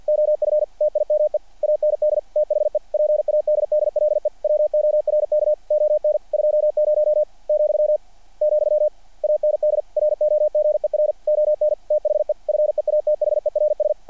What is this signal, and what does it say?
Morse code, heard as a keyed tone, about 48 wpm, 600 Hz. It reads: OF TIME AND THE PADDLE WORK ON 20 , , AND ROGER ON THE FEATHERS
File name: archived / CRSnet